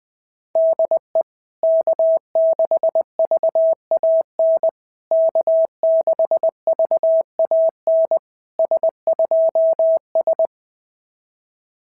Morse code, heard as a keyed tone, about 20 wpm, 650 Hz. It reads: DE K6VAN K6VAN S2S
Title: SOTA QSO Practice 20wpm